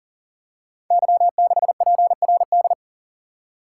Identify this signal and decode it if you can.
Morse code, heard as a keyed tone, about 42 wpm, 690 Hz. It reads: Y6PRD
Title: Code Groups - 5 Letter:Numbers - 40wpm